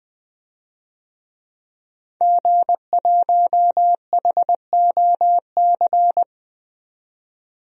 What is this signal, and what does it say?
Morse code, heard as a keyed tone, about 21 wpm, 700 Hz. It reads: G1HOC
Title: DXCC-100 Call Signs - 5 Characters - 20wpm